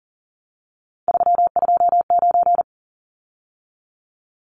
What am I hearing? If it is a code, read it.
Morse code, heard as a keyed tone, about 40 wpm, 700 Hz. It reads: 329
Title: Numbers - 3-Digit 40wpm